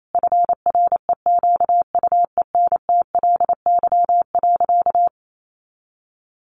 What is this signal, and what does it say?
Morse code, heard as a keyed tone, about 28 wpm, 700 Hz. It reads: FREQUENTLY.
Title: Sentences from Top 1300 Words 28wpm